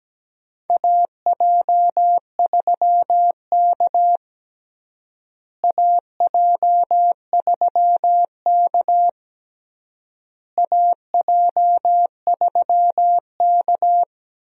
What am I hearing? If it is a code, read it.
Morse code, heard as a keyed tone, about 17 wpm, 700 Hz. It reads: AJ3K AJ3K AJ3K
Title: USA Call Signs - 4 Characters - Repeated 3x - 17wpm